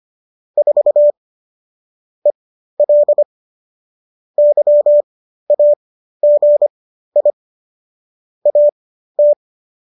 Morse code, heard as a keyed tone, about 25 wpm, spaced as for 13 wpm, 590 Hz. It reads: 4 EL YAGI AT
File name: qso207 13